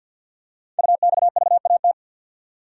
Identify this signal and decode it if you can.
Morse code, heard as a keyed tone, about 47 wpm, 700 Hz. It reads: UXVAT
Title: Code Groups - 5 Letters - 50wpm